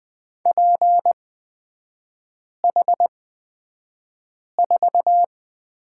Morse code, heard as a keyed tone, about 20 wpm, spaced as for 5 wpm, 700 Hz. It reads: PH4
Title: 236 Lesson 38e - Letter q - Three Characters 20wpm